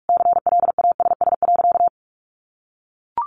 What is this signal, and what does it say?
Morse code, heard as a keyed tone, about 44 wpm, 700 Hz. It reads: CLASS.
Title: Sentences from Top 1400 Words 45wpm